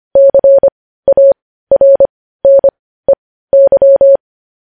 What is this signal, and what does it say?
Morse code, heard as a keyed tone, about 25 wpm, spaced as for 15 wpm, 550 Hz. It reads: CARNEY
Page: news headlines in morse code at 15 wpm